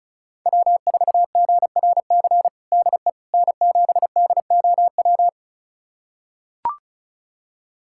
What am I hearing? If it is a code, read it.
Morse code, heard as a keyed tone, about 36 wpm, 700 Hz. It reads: W4GRC DE N7DOW
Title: USA Callsign DE Callsign - 5 Characters - 35wpm